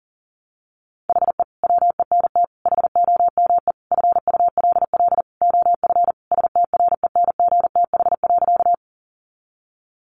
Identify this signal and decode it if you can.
Morse code, heard as a keyed tone, about 37 wpm, 700 Hz. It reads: HE WENT HOME FULL OF STRENGTH.
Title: Sentences from Top 1300 Words 40wpm